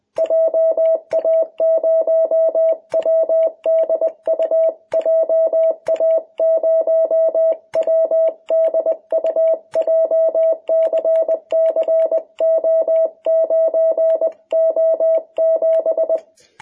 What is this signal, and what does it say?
Morse code, heard as a keyed tone, about 20 wpm, 660 Hz. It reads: JA0WDU JA0WDU JCCO9O7
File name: jh8cbh / 2021cw